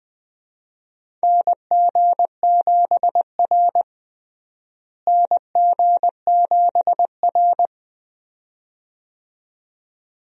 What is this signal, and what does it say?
Morse code, heard as a keyed tone, about 20 wpm, 700 Hz. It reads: NG7R NG7R
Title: USA Call Signs - 4 Characters - Repeated 2x - 20wpm